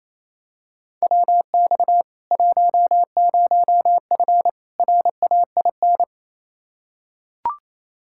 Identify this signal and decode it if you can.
Morse code, heard as a keyed tone, about 28 wpm, 700 Hz. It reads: WX 10F RAIN
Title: K1USN SST QSO Practice 28wpm